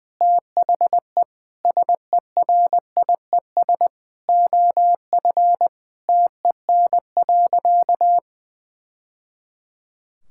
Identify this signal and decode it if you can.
Morse code, heard as a keyed tone, about 20 wpm, 700 Hz. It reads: THE SERIES OF TEN.
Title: Sentences from Top 1000 Words - Review Part 5 - 20wpm